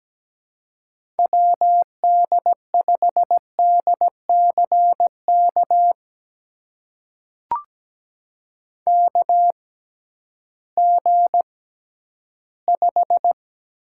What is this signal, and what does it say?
Morse code, heard as a keyed tone, about 17 wpm, spaced as for 11 wpm, 700 Hz. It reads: WD5DCK KG5
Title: USA Call Signs - 6 Characters - 6x Character Spacing - 17wpm